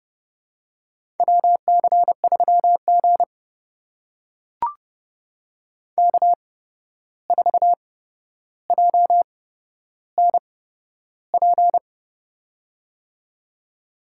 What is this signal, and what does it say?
Morse code, heard as a keyed tone, about 30 wpm, spaced as for 12 wpm, 700 Hz. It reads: WC3G K4JNP
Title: USA Call Signs - 8x Character Spacing - 30wpm